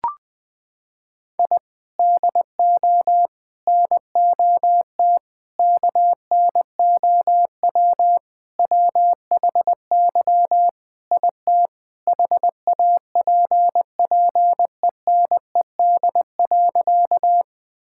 Morse code, heard as a keyed tone, about 20 wpm, 700 Hz. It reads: I DO NOT KNOW WHY IT HAPPENED.
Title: Sentences with 5-Word Phrases - 20wpm